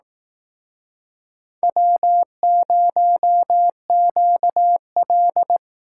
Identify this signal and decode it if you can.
Morse code, heard as a keyed tone, about 18 wpm, 700 Hz. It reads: W0QL
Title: Speed Racing - USA 1x2 Call Signs 15wpm